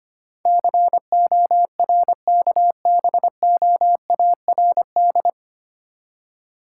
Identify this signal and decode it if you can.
Morse code, heard as a keyed tone, about 25 wpm, 700 Hz. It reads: CORKBOARD
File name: practice / 25wpm / Difficult Words - Speed-Racing with Hints - 25wpm